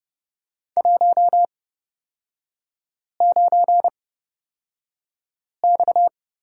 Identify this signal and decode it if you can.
Morse code, heard as a keyed tone, about 30 wpm, spaced as for 5 wpm, 700 Hz. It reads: 19X
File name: Lesson 38e - Letter q - Three Characters 30wpm